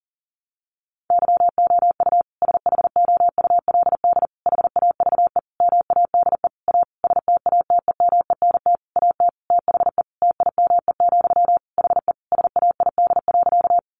Morse code, heard as a keyed tone, about 40 wpm, 700 Hz. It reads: YOU SHOULD HAVE MADE A STATEMENT AT THE TIME, HE SAID.
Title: Sentences from Top 1200 Words 40wpm